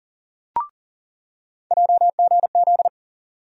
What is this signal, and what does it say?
Morse code, heard as a keyed tone, about 40 wpm, 700 Hz. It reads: JGZ
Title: USA Call Sign Postfixes - 3 Letters 40wpm